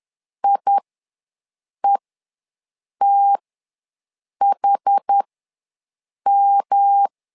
Morse code, heard as a keyed tone, about 10 wpm, spaced as for 6 wpm, 800 Hz. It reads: IETHM
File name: LEZIONI / 3 MWRO / 3 M ws 10wpm 2121